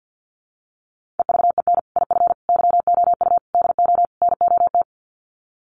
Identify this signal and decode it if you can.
Morse code, heard as a keyed tone, about 50 wpm, 700 Hz. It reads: EVEN IF YOU DO NOT
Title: Sets of 5 Words Encore 50wpm